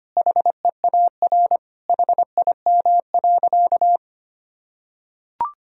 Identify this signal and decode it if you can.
Morse code, heard as a keyed tone, about 25 wpm, 700 Hz. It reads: HEAR HIM.
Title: Sentences from Top 1000 Words - Review Part 1 - 25wpm